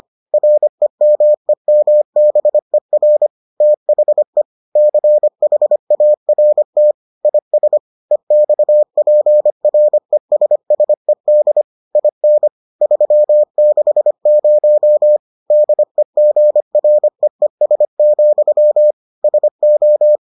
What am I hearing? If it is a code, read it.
Morse code, heard as a keyed tone, about 25 wpm, 600 Hz. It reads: REMEMBER THE CHART IS EXPRESSED IN 360 DEGREES, SO